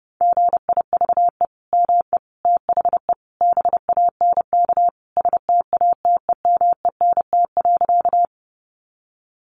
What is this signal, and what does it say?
Morse code, heard as a keyed tone, about 30 wpm, 700 Hz. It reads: GIVE ME THE BANK STATEMENT.
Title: Sentences from Top 1200 Words 30wpm